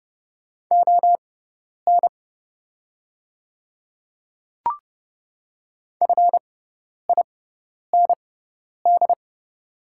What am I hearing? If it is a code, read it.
Morse code, heard as a keyed tone, about 30 wpm, spaced as for 10 wpm, 700 Hz. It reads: ON FIND
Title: 6x Character Spacing - Top 100 Words - 30wpm